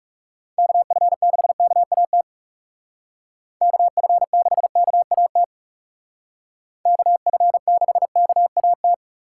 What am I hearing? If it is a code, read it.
Morse code, heard as a keyed tone, about 39 wpm, 700 Hz. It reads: KF6KAT KF6KAT KF6KAT